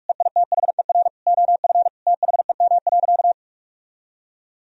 Morse code, heard as a keyed tone, about 46 wpm, 700 Hz. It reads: EITHER OF THEM.
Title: Sentences from Top 1000 Words - Review Part 4 - 45wpm